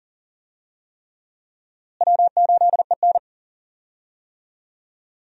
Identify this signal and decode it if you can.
Morse code, heard as a keyed tone, about 40 wpm, 700 Hz. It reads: W8EN
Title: USA Call Signs - 4 Characters - Repeated 3x - 40wpm